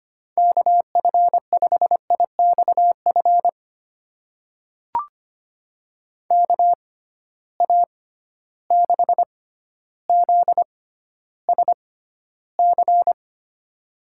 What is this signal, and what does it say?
Morse code, heard as a keyed tone, about 25 wpm, spaced as for 13 wpm, 700 Hz. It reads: KF5IXF KA6ZSC
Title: USA Call Signs - 6 Characters - 6x Character Spacing - 25wpm